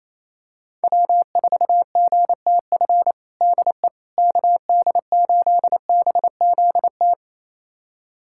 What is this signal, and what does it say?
Morse code, heard as a keyed tone, about 28 wpm, 700 Hz. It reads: W4GTF DE KD8BZT
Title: USA Callsign DE Callsign 28wpm